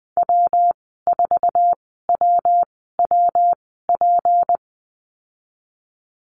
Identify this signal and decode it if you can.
Morse code, heard as a keyed tone, about 19 wpm, spaced as for 14 wpm, 700 Hz. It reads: W4WWP
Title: 173 Lesson 28g - Number 9 - Call Signs 20wpm